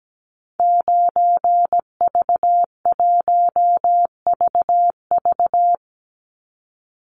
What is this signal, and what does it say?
Morse code, heard as a keyed tone, about 17 wpm, 700 Hz. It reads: 9V1VV
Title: Call Signs Repeated 2x - 17wpm